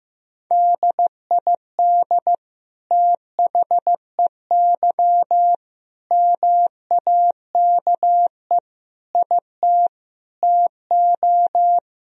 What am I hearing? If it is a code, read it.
Morse code, heard as a keyed tone, about 15 wpm, 700 Hz. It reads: DID THEY MAKE IT TO
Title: Sentences from Top 1000 Words - Review Part 3 - 15wpm